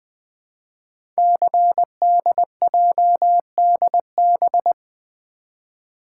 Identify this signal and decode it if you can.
Morse code, heard as a keyed tone, about 20 wpm, 700 Hz. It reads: CDJDB
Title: Code Groups - 5 Letters - 20wpm